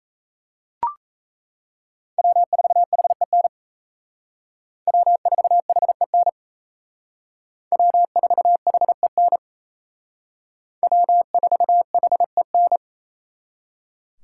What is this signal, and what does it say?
Morse code, heard as a keyed tone, about 37 wpm, 700 Hz. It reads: W4HEN W4HEN W4HEN W4HEN